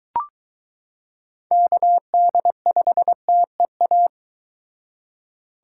Morse code, heard as a keyed tone, about 22 wpm, 700 Hz. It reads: KD5TEA